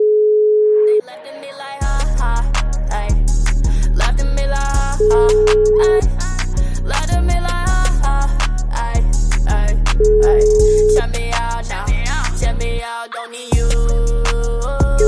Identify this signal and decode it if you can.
Ringback tone (Europe, 425 Hz).